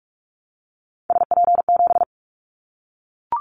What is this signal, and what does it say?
Morse code, heard as a keyed tone, about 44 wpm, 700 Hz. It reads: SP7
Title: Callsign Prefixes for Top 25 DXCC - Repeated 3x - 45wpm